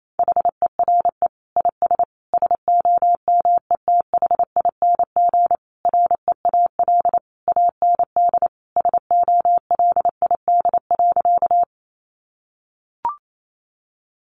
Morse code, heard as a keyed tone, about 28 wpm, 700 Hz. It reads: HERE IS SOMETHING REAL AND SOLID.
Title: Sentences from Top 1500 Words 28wpm